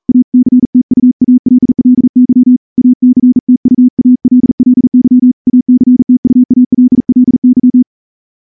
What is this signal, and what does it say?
Morse code, heard as a keyed tone, about 38 wpm, 250 Hz. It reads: ACTUALLY ACTUALLY ACTUALLY
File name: morse-training-40WPM